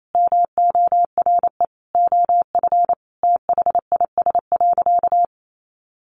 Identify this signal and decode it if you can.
Morse code, heard as a keyed tone, about 28 wpm, 700 Hz. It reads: MORE OF THIS.
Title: Sentences from Top 1100 Words 28wpm